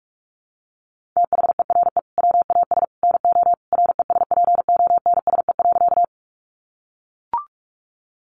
Morse code, heard as a keyed tone, about 44 wpm, 700 Hz. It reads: THERE WAS NO RESPONSE.
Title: Sentences from Top 1400 Words 45wpm